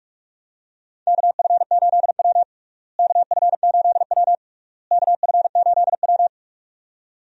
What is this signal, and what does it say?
Morse code, heard as a keyed tone, about 41 wpm, 700 Hz. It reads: KF8W KF8W KF8W
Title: USA Call Signs - 4 Characters - Repeated 3x - 45wpm